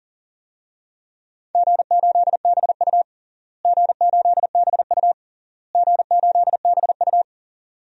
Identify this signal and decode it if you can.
Morse code, heard as a keyed tone, about 37 wpm, 700 Hz. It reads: G8BU G8BU G8BU
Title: DXCC-100 Call Signs - 4 Characters - Repeated 3x - 40wpm